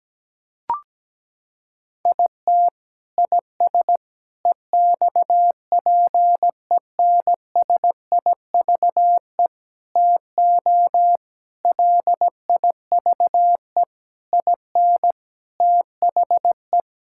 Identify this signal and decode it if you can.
Morse code, heard as a keyed tone, about 17 wpm, 700 Hz. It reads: IT IS EXPENSIVE TO LIVE IN THE
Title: Sentences from Top 1000 Words - Review Part 6 - 17wpm